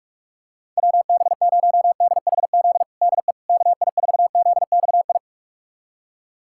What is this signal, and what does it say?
Morse code, heard as a keyed tone, about 46 wpm, 700 Hz. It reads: WB0DHZ DE KI4ZXI
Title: USA Callsign DE Callsign - 6 Characters - 45wpm